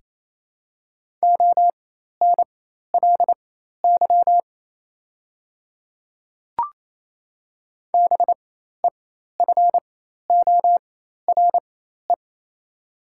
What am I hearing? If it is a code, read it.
Morse code, heard as a keyed tone, about 28 wpm, spaced as for 13 wpm, 700 Hz. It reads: ONLY BEFORE
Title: 4x Character Spacing - Top 200 Words - 28wpm